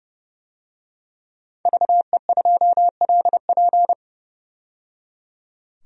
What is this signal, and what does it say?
Morse code, heard as a keyed tone, about 32 wpm, 700 Hz. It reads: VE2LP
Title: DXCC-100 Call Signs - 5 Characters - 30wpm